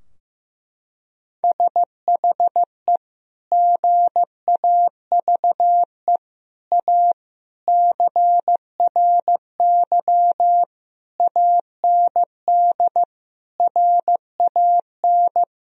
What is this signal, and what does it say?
Morse code, heard as a keyed tone, about 15 wpm, 700 Hz. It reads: SHE GAVE A CRY AND RAN
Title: Sentences from Top 1000 Words - Review Part 3 - 15wpm